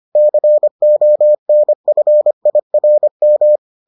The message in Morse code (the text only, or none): CONFIRM